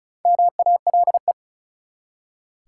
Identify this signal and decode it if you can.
Morse code, heard as a keyed tone, about 36 wpm, 700 Hz. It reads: MALE
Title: Four Letter Words - 35wpm